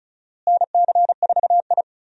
Morse code, heard as a keyed tone, about 36 wpm, 700 Hz. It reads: NC4I
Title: USA Call Signs - 4 Characters - Repeated 2x - 35wpm